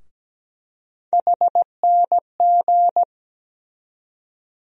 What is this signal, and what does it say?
Morse code, heard as a keyed tone, about 17 wpm, 700 Hz. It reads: HNG
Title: USA Call Sign Postfixes - 3 Letters 17wpm